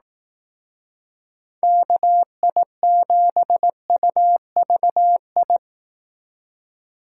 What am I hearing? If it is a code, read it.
Morse code, heard as a keyed tone, about 18 wpm, 700 Hz. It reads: KI7UVI
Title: Speed Racing - USA Call Signs 15wpm